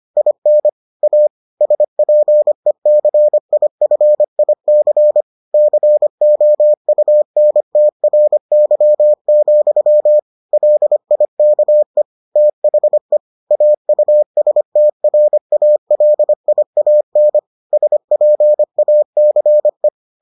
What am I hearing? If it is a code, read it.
Morse code, heard as a keyed tone, about 25 wpm, 600 Hz. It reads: IN A SPECIFIC COUNTRY, LIKE THE AUSTRALIAN SPACE